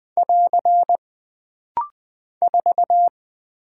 Morse code, heard as a keyed tone, about 20 wpm, 700 Hz. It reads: <AR> 4
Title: Single Character (Letter:Number:Punctuation:Prosign) - Rapid-Fire - 20wpm